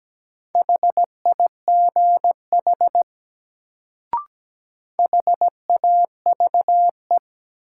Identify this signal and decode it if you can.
Morse code, heard as a keyed tone, about 17 wpm, 700 Hz. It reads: HIGH HAVE
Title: Sets of 2 Words - New Word Drill - 17wpm